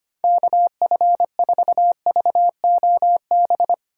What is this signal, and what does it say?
Morse code, heard as a keyed tone, about 25 wpm, 700 Hz. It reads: KF4VOB
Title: USA Call Signs - 6 Characters - 4x Character Spacing - 25wpm